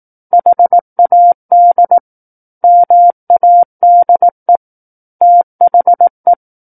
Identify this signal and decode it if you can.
Morse code, heard as a keyed tone, about 18 wpm, 700 Hz. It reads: HAD MADE THE
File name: Book One / 09-Chapter 9 - 18 WPM